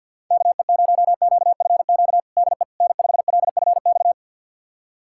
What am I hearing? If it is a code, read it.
Morse code, heard as a keyed tone, about 50 wpm, 700 Hz. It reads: KE0QFQ DE N5LFX
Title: USA Callsign DE Callsign 50wpm